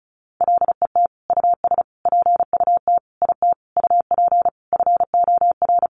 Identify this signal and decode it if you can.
Morse code, heard as a keyed tone, about 34 wpm, 700 Hz. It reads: LET US PUT IT UP FOR
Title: Sentences from Top 1400 Words 35wpm